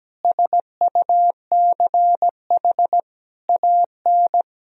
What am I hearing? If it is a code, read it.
Morse code, heard as a keyed tone, about 17 wpm, 700 Hz. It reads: SUCH AN
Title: Sentences from Top 400 Words and Abbreviations 17wpm